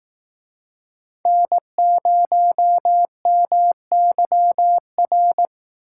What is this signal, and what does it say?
Morse code, heard as a keyed tone, about 18 wpm, 700 Hz. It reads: N0MYR
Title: Speed Racing - USA 1x3 Call Signs 15wpm